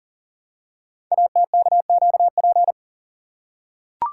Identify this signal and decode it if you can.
Morse code, heard as a keyed tone, about 42 wpm, 700 Hz. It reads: ATKQP
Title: Code Groups - 5 Letters - 40wpm